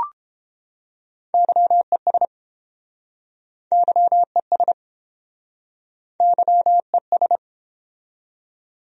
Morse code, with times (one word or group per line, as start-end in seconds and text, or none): 1.3-2.2 YES
3.7-4.7 YES
6.2-7.4 YES